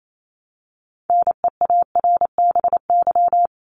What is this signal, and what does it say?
Morse code, heard as a keyed tone, about 28 wpm, 700 Hz. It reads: NEARBY
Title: Top 1501-1750 Words 28wpm